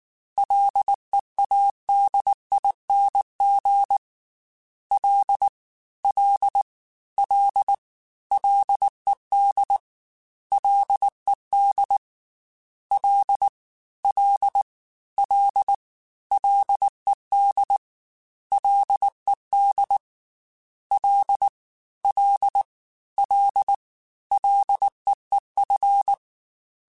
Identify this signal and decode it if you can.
Morse code, heard as a keyed tone, about 19 wpm, 800 Hz. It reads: LEADING L L L LED LED L L L LED LED L L L LEEF
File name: F and L / F and L mp3 / F and L Prefixes 3x2 Twice @18